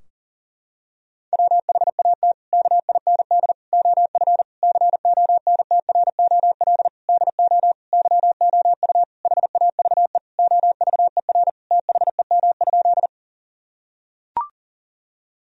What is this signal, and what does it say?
Morse code, heard as a keyed tone, about 40 wpm, 700 Hz. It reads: WHAT KIND OF CONTROL DO YOU HAVE OVER THEM?
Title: Sentences from Top 1000 Words - Review Part 3 - 40wpm